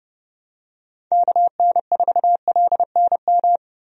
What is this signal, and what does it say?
Morse code, heard as a keyed tone, about 30 wpm, 700 Hz. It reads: KN4LNM
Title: USA Call Signs - 6 Characters - Repeated 2x - 30wpm